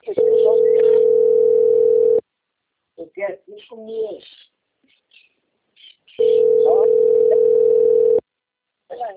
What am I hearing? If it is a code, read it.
Ringback tone (North America), 440 + 480 Hz, with speech.
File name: out- - - - - .63563